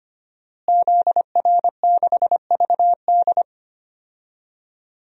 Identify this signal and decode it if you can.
Morse code, heard as a keyed tone, about 25 wpm, 700 Hz. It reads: ZR6VD
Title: DXCC-100 Call Signs - 5 Characters - Repeated 2x - 25wpm